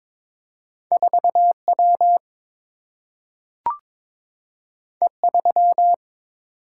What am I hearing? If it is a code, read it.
Morse code, heard as a keyed tone, about 22 wpm, 700 Hz. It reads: 4 W E 3
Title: Single-Letter Single-Number 22wpm